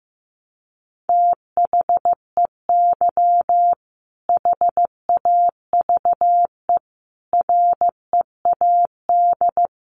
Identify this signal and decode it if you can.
Morse code, heard as a keyed tone, about 14 wpm, 700 Hz. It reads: THEY HAVE READ
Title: Sentences from Top 1300 Words 15wpm